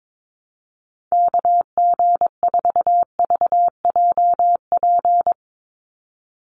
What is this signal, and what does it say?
Morse code, heard as a keyed tone, about 22 wpm, 700 Hz. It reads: KG4VJP
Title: USA Call Signs 22wpm